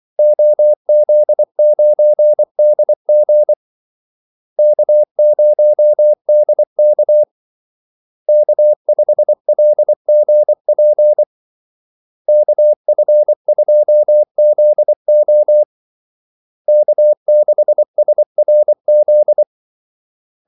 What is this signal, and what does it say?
Morse code, heard as a keyed tone, about 24 wpm, 600 Hz. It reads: OZ9DG K0DK K5LGP KF2ZO K6SRZ
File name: Callsigns-24wpm